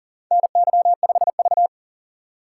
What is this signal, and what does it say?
Morse code, heard as a keyed tone, about 37 wpm, 700 Hz. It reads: NY5V
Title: USA Call Signs - 4x Character Spacing - 40wpm